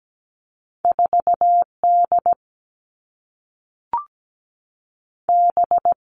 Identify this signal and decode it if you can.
Morse code, heard as a keyed tone, about 17 wpm, 700 Hz. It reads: 4D B
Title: All Cut Numbers - 2-Digit 17wpm